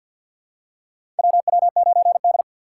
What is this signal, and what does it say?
Morse code, heard as a keyed tone, about 53 wpm, 690 Hz. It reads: WW9D
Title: USA Call Signs - 4 Characters - 6x Character Spacing - 50wpm